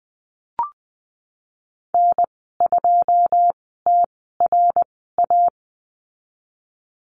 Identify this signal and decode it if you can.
Morse code, heard as a keyed tone, about 20 wpm, spaced as for 15 wpm, 700 Hz. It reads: N2TRA
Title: 186 Lesson 30g - Letter b - Call Signs 20wpm